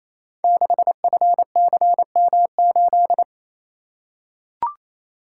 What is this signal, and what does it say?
Morse code, heard as a keyed tone, about 28 wpm, 700 Hz. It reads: 6FCM8
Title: Code Groups - 5 Letter:Numbers - 28wpm